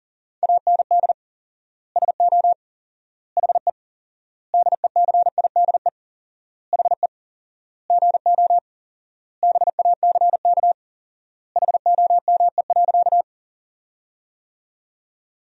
Morse code, heard as a keyed tone, about 40 wpm, 700 Hz. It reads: AND SO HE DECIDE HE GO BACK HOME.
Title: 4x Word-Spacing - Sentences from Top 500 Words - 40wpm